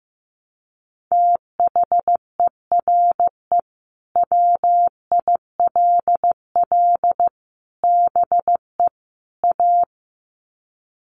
Morse code, heard as a keyed tone, about 15 wpm, 700 Hz. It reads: THERE WILL BE A
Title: Sets of 4 Words 15wpm